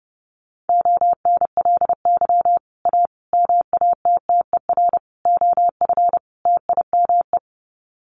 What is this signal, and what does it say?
Morse code, heard as a keyed tone, about 30 wpm, 700 Hz. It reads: ONLY A MATTER OF TIME
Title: Sets of 5 Words Encore 30wpm